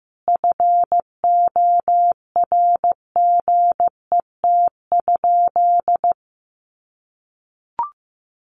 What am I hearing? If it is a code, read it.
Morse code, heard as a keyed tone, about 15 wpm, 700 Hz. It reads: FORGET?
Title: Sentences from Top 1200 Words 15wpm